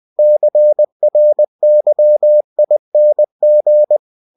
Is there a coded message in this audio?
Yes: Morse code from 0.2 to 4.0 s, heard as a keyed tone, about 20 wpm, 600 Hz. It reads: CRYING